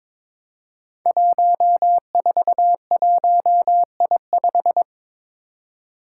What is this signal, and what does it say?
Morse code, heard as a keyed tone, about 21 wpm, 700 Hz. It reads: 141I5
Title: Five Challenging Letter-Number Combinations 22wpm